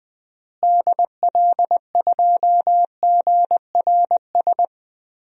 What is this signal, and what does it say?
Morse code, heard as a keyed tone, about 20 wpm, 700 Hz. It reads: DL2GRS
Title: DXCC-100 Call Signs - 6 Characters - Repeated 3x - 20wpm